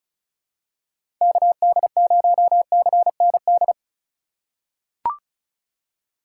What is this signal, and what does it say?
Morse code, heard as a keyed tone, about 34 wpm, 700 Hz. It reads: KD0CND
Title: USA Call Signs - 6 Characters - 2x Character Spacing - 35wpm